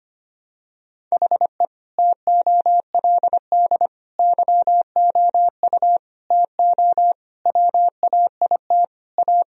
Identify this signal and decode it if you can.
Morse code, heard as a keyed tone, about 25 wpm, 700 Hz. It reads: HE TOLD YOU TO WAIT A
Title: Sentences from Top 1000 Words - Review Part 2 - 25wpm